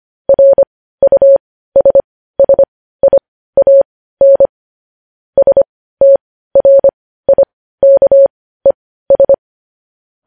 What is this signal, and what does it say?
Morse code, heard as a keyed tone, about 25 wpm, spaced as for 15 wpm, 550 Hz. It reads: RUSSIAN STRIKES
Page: News Headlines in Morse Code at 15 WPM